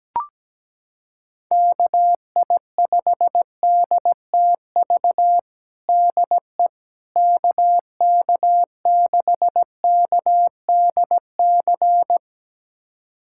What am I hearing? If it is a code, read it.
Morse code, heard as a keyed tone, about 17 wpm, 700 Hz. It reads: KI5DTV DE KK6KDC
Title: USA Callsign DE Callsign - 6 Characters - 17wpm